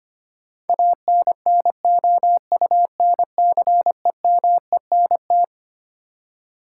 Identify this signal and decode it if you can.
Morse code, heard as a keyed tone, about 25 wpm, 700 Hz. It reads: ANNOUNCEMENT
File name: Twelve Letter Words - 25wpm